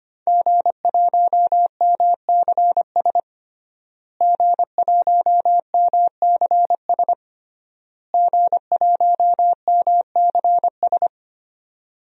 Morse code, heard as a keyed tone, about 25 wpm, 700 Hz. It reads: G1MCS G1MCS G1MCS
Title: DXCC-100 Call Signs - 5 Characters - Repeated 3x - 25wpm